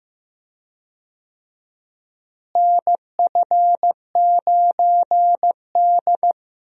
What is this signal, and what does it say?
Morse code, heard as a keyed tone, about 15 wpm, 700 Hz. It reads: NF9D
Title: USA Call Signs - 4 Characters - Repeated 2x - 15wpm